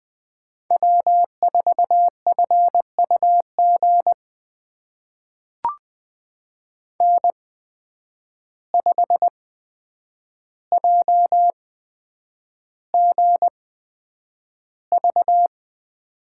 Morse code, heard as a keyed tone, about 20 wpm, spaced as for 9 wpm, 700 Hz. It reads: W4FUG N5JGV
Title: USA Call Signs - 5 Characters - 8x Character Spacing - 20wpm